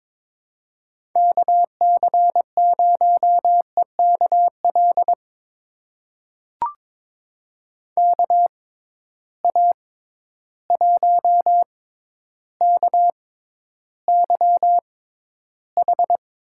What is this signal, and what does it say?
Morse code, heard as a keyed tone, about 22 wpm, spaced as for 11 wpm, 700 Hz. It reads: KC0EKL KA1KYH
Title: USA Call Signs - 6 Characters - 6x Character Spacing - 22wpm